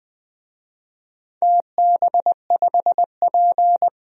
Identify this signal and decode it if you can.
Morse code, heard as a keyed tone, about 21 wpm, 700 Hz. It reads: TB5P
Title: DXCC-100 Call Signs - 4 Characters - 20wpm